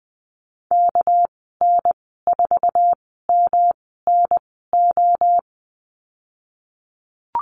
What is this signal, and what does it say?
Morse code, heard as a keyed tone, about 20 wpm, spaced as for 15 wpm, 700 Hz. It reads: KN4MNO
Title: 207 Lesson 33g - Letter j - Call Signs 20wpm